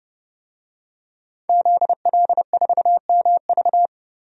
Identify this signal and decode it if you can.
Morse code, heard as a keyed tone, about 30 wpm, 700 Hz. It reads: ZL4MV
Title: DXCC-100 Call Signs - 5 Characters - 30wpm